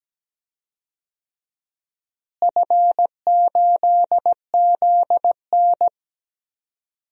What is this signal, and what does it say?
Morse code, heard as a keyed tone, about 17 wpm, 700 Hz. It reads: F8ZN
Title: DXCC-100 Call Signs - 4 Characters - Repeated 3x - 17wpm